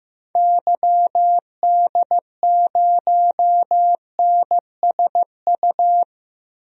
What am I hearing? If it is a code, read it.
Morse code, heard as a keyed tone, about 15 wpm, 700 Hz. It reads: YD0NSU
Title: DXCC-100 Call Signs - 6 Characters - Repeated 3x - 15wpm